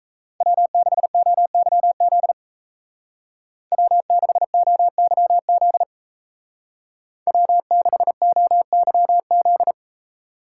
Morse code, heard as a keyed tone, about 32 wpm, 700 Hz. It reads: W6OYZ W6OYZ W6OYZ
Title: Speed Racing - USA 1x3 Call Signs 28wpm